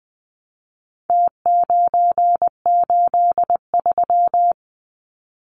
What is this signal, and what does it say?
Morse code, heard as a keyed tone, about 20 wpm, 700 Hz. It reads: T983
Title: Numbers - Contest Style - 4-Digit 20wpm